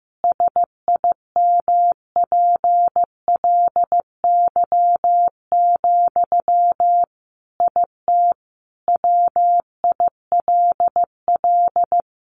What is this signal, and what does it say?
Morse code, heard as a keyed tone, about 15 wpm, 700 Hz. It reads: SIMPLY, IT WILL
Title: Sentences from Top 800 Words 15wpm